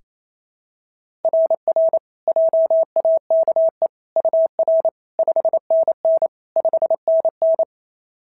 Morse code, heard as a keyed tone, about 27 wpm, 660 Hz. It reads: RR JAKE UR 5NN 5NN
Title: SOTA QSO Practice 28wpm